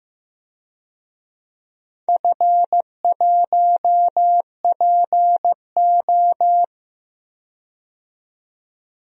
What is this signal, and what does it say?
Morse code, heard as a keyed tone, about 15 wpm, 700 Hz. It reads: F1PO
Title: DXCC-100 Call Signs - 4 Characters - 15wpm